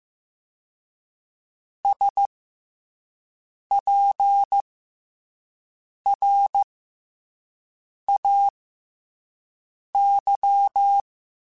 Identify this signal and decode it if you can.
Morse code, heard as a keyed tone, about 15 wpm, 800 Hz. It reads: S P R A Y
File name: Special INGs-mp3s / Special INGs-mp3s / SlowToFaster ing @10